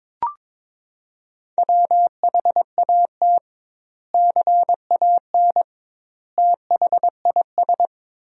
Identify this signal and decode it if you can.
Morse code, heard as a keyed tone, about 22 wpm, 700 Hz. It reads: WHAT CAN THIS
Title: 2x Word-Spacing - Sentences from Top 100 Words - 22wpm